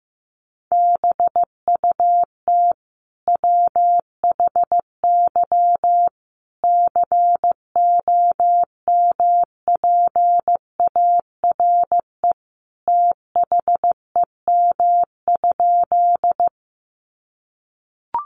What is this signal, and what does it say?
Morse code, heard as a keyed tone, about 15 wpm, 700 Hz. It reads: BUT WHY COMPARE THEM?
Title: Sentences from Top 1300 Words 15wpm